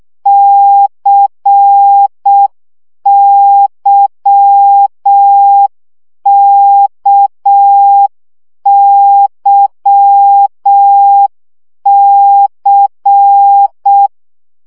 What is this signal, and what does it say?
Morse code, heard as a keyed tone, about 6 wpm, 800 Hz. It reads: CYKYC
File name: CW gyakorlas k c y 5 baud